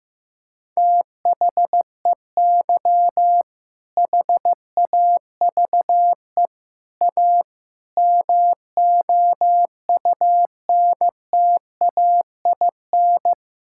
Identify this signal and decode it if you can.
Morse code, heard as a keyed tone, about 15 wpm, 700 Hz. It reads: THEY HAVE A MOUNTAIN
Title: Sentences from Top 1000 Words - Review Part 5 - 15wpm